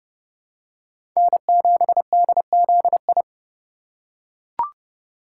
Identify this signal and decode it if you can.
Morse code, heard as a keyed tone, about 30 wpm, 700 Hz. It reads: N7DZI
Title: USA Call Signs - 5 Characters - 2x Character Spacing - 30wpm